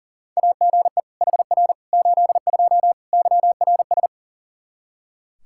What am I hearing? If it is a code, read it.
Morse code, heard as a keyed tone, about 40 wpm, 700 Hz. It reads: AGE HR 82 YRS
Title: K1USN SST QSO Practice 40wpm